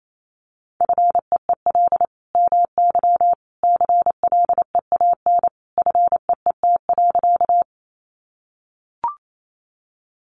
Morse code, heard as a keyed tone, about 28 wpm, 700 Hz. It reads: FEEL MY CLEAN FEET.
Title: Sentences from Top 1100 Words 28wpm